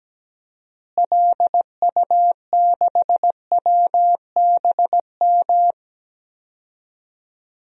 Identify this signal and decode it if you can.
Morse code, heard as a keyed tone, about 17 wpm, 700 Hz. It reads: LU6WBM
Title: DXCC-100 Call Signs - 6 Characters - Repeated 2x - 17wpm